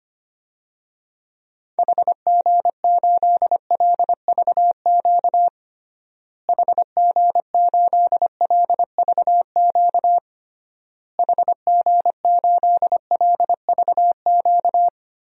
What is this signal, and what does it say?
Morse code, heard as a keyed tone, about 25 wpm, 700 Hz. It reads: HG8LVQ HG8LVQ HG8LVQ
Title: DXCC-100 Call Signs - 6 Characters - Repeated 3x - 25wpm